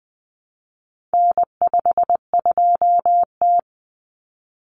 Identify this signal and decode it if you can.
Morse code, heard as a keyed tone, about 20 wpm, 700 Hz. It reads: N52T
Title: Common Cut Numbers - 4-Digit 20wpm